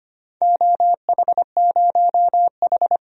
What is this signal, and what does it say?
Morse code, heard as a keyed tone, about 25 wpm, 700 Hz. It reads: OH0H
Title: DXCC-100 Call Signs - 4 Characters - 25wpm